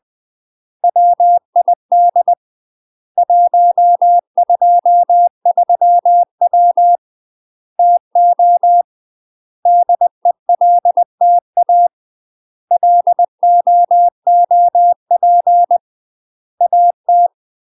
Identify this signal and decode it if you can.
Morse code, heard as a keyed tone, about 20 wpm, 700 Hz. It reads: WID 123W TO DELTA LOOP AT
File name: UK MASTER QSOs TRAINING / 20WPM / UK MASTER QSOs 0002